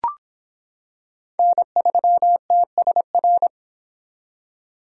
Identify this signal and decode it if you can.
Morse code, heard as a keyed tone, about 25 wpm, 700 Hz. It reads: N3TSR